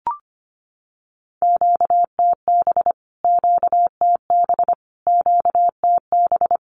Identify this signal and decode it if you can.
Morse code, heard as a keyed tone, about 25 wpm, 700 Hz. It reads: QTB QTB QTB
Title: Q-Codes Repeated 3x 25wpm